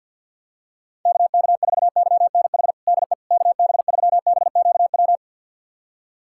Morse code, heard as a keyed tone, about 51 wpm, 700 Hz. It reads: KK4YNH DE KB3BQW
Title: USA Callsign DE Callsign - 6 Characters - 50wpm